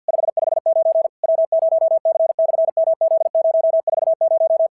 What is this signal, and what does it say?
Morse code, heard as a keyed tone, about 50 wpm, 650 Hz. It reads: 559 W0C/MZ040
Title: SOTA QSO Practice 50wpm